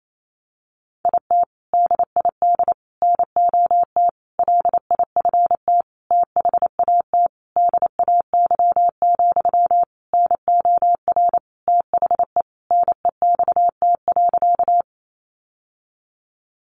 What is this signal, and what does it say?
Morse code, heard as a keyed tone, about 28 wpm, 700 Hz. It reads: IT DID NOT LIFT THAT DAY, NOR THE NEXT.
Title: Sentences from Top 1500 Words 28wpm